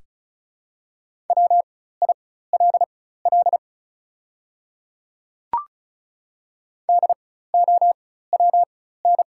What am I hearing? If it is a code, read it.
Morse code, heard as a keyed tone, about 36 wpm, spaced as for 16 wpm, 690 Hz. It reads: WILL DOWN
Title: 4x Character Spacing - Top 300 Words - 35wpm